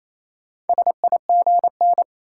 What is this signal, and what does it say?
Morse code, heard as a keyed tone, about 28 wpm, 700 Hz. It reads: SIGN
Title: Sentences from Top 1000 Words - New Word Drill - 28wpm